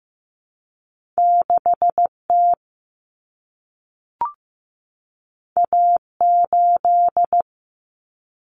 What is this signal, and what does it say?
Morse code, heard as a keyed tone, about 15 wpm, 700 Hz. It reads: 6T A8
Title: Common Cut Numbers - 2-Digit 15wpm